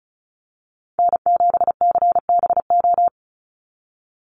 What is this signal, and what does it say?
Morse code, heard as a keyed tone, about 34 wpm, 700 Hz. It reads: N7CBO
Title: USA Call Signs 35wpm